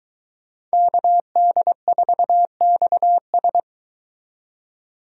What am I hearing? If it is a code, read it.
Morse code, heard as a keyed tone, about 22 wpm, 700 Hz. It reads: KD4XS